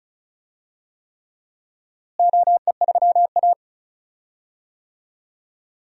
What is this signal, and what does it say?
Morse code, heard as a keyed tone, about 36 wpm, 700 Hz. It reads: OE3A
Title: DXCC-100 Call Signs - 4 Characters - Repeated 3x - 35wpm